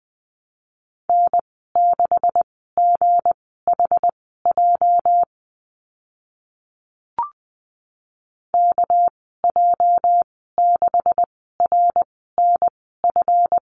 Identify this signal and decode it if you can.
Morse code, heard as a keyed tone, about 20 wpm, spaced as for 14 wpm, 700 Hz. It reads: N6GHJ KJ6RNF
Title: 213 Lesson 34g - Number 8 - Call Signs 20wpm